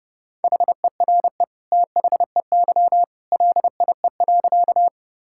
Morse code, heard as a keyed tone, about 32 wpm, 700 Hz. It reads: HERE THEY LIE.
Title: Sentences from Top 1600 Words 30wpm